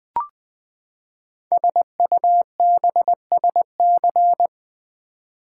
Morse code, heard as a keyed tone, about 20 wpm, 700 Hz. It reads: SUBSC
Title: Build A Word - 5 Letters At A Time - 12-Letter Words - 20wpm